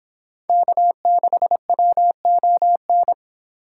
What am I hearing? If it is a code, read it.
Morse code, heard as a keyed tone, about 26 wpm, 700 Hz. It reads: K6WON